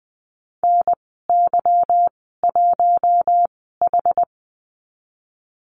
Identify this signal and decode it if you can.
Morse code, heard as a keyed tone, about 20 wpm, spaced as for 15 wpm, 700 Hz. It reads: NY1H
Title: 193 Lesson 31g - Letter v - Call Signs 20wpm